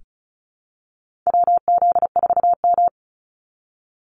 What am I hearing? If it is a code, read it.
Morse code, heard as a keyed tone, about 34 wpm, 700 Hz. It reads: WZ4M
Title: NA SOTA Activator Callsigns 35wpm